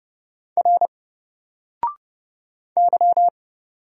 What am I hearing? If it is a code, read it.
Morse code, heard as a keyed tone, about 31 wpm, 700 Hz. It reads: R Y
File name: Single-Letter-Number-30wpm